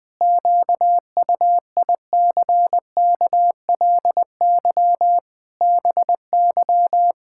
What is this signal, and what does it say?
Morse code, heard as a keyed tone, about 21 wpm, 700 Hz. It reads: QUICKLY BY
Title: Pangram Challenge v2 - 20wpm